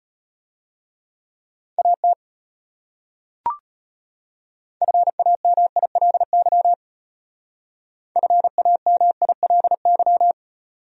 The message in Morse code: AT FAMILY FAMILY

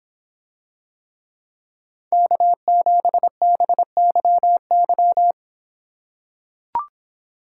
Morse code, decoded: K7BYY

